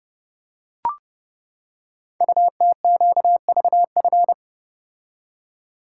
Morse code, heard as a keyed tone, about 29 wpm, 700 Hz. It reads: UTQVF
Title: Code Groups - 5 Letters - 30wpm